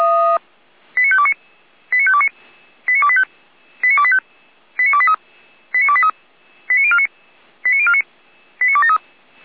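Five-tone selective call (ZVEI), heard as quick runs of five tones, tones of about 70 ms. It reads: 89529 89529 89285 89285 89283 89283 89049 89049 89273
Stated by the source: ZVEI five-tone selective call received by SDR.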